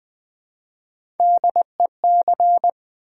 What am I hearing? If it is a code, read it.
Morse code, heard as a keyed tone, about 20 wpm, 700 Hz. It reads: DEC
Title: Build A Word - 3 Letters At A Time - 8-Letter Words - 20wpm